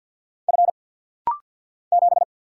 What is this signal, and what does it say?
Morse code, heard as a keyed tone, about 48 wpm, 690 Hz. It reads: F 7
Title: Single Letter-Number - Rapid-Fire 50wpm